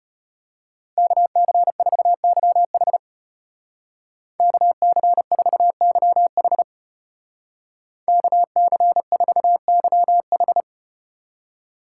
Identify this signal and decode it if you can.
Morse code, heard as a keyed tone, about 35 wpm, 700 Hz. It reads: KC4YH KC4YH KC4YH